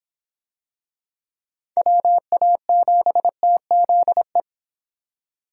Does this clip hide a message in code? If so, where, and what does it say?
Yes: Morse code from 1.8 to 4.4 s, heard as a keyed tone, about 25 wpm, 700 Hz. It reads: WA7TZE